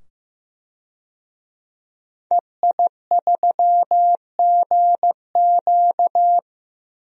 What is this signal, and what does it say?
Morse code, heard as a keyed tone, about 15 wpm, 700 Hz. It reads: EI3GQ
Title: DXCC-100 Call Signs - 5 Characters - 15wpm